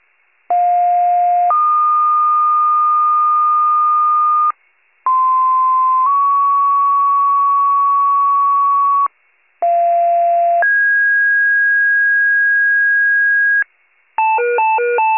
Two-tone page, heard as a one-second tone, then a three-second tone, about 710 then 1220 Hz, about 1040 then 1120 Hz, about 690 then 1650 Hz, with noise. It is heard from 0.5 to 13.6 s.
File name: pager / 46 RESCUE / wcalert